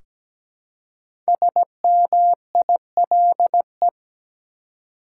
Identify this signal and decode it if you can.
Morse code, heard as a keyed tone, about 17 wpm, 700 Hz. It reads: SMILE
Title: Five Letter Words - 17wpm